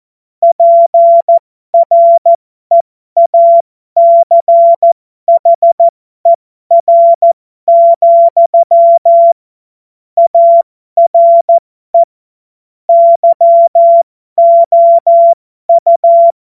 Morse code, heard as a keyed tone, about 14 wpm, spaced as for 12 wpm, 680 Hz. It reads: PREACHER, ARE YOU